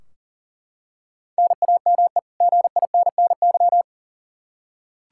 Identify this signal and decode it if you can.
Morse code, heard as a keyed tone, about 40 wpm, 700 Hz. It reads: NAME GINNY
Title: Practice QSO - Exchange 1 - Beginning - 40wpm